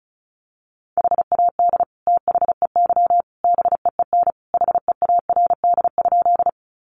Morse code, heard as a keyed tone, about 34 wpm, 700 Hz. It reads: HAD THEY BEEN HEARD?
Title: Sentences from Top 400 Words 35wpm